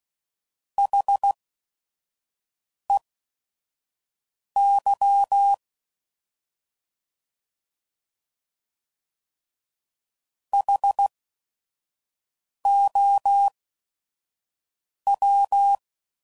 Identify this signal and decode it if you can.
Morse code, heard as a keyed tone, about 16 wpm, spaced as for 5 wpm, 800 Hz. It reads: HEY HOW
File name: Prefixes / By First Letter / Hs